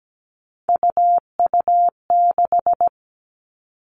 Morse code, heard as a keyed tone, about 17 wpm, 700 Hz. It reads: UU6
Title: All Cut Numbers - 3-Digit 17wpm